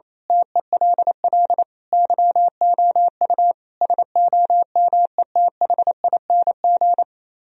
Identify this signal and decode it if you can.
Morse code, heard as a keyed tone, about 26 wpm, 700 Hz. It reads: TELL YOU SOMETHING
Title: Speed-Racing - Sets of 5 Words 28wpm